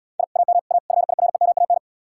Morse code, heard as a keyed tone, about 77 wpm, 700 Hz. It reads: I AM A FIREMAN